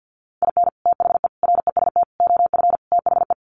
Morse code, heard as a keyed tone, about 50 wpm, 700 Hz. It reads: IN THE REST OF THE
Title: Sets of 5 Words 50wpm